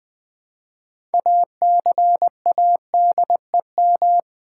Morse code, heard as a keyed tone, about 20 wpm, 700 Hz. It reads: ACADEM
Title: Build A Word - 2 Letters At A Time - 8-Letter Words - 20wpm